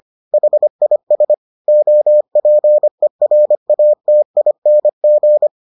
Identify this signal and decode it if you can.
Morse code, heard as a keyed tone, about 25 wpm, 600 Hz. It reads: HIS OPERATING